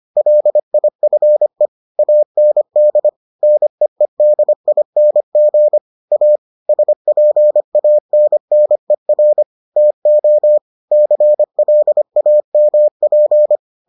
Morse code, heard as a keyed tone, about 25 wpm, 600 Hz. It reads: LIFE AND NEEDING A SPANNER TO CLAMP